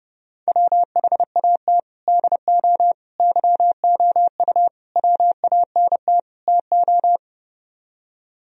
Morse code, heard as a keyed tone, about 30 wpm, 700 Hz. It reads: WHAT DO YOU WANT TO